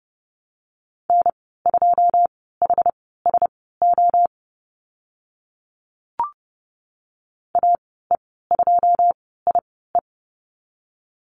Morse code, heard as a keyed tone, about 29 wpm, spaced as for 17 wpm, 700 Hz. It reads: N2HSO AE2IE
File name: Lesson 14g - Number 2 - Call Signs 30wpm